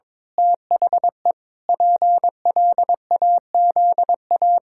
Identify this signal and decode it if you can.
Morse code, heard as a keyed tone, about 22 wpm, 700 Hz. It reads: THE PLAZA
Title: Pangram Challenge v2 - 22wpm